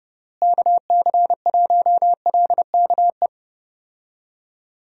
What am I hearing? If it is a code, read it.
Morse code, heard as a keyed tone, about 30 wpm, 700 Hz. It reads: KC1LKE
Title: USA Call Signs - 6 Characters - Repeated 3x - 30wpm